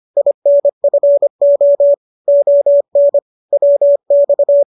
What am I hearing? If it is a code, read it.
Morse code, heard as a keyed tone, about 25 wpm, 570 Hz. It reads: INFO ON WX
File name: qso221 25